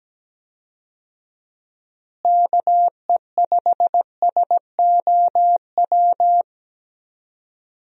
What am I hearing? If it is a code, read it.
Morse code, heard as a keyed tone, about 17 wpm, 700 Hz. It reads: KE5SOW
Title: USA Call Signs - 6 Characters - Repeated 3x - 17wpm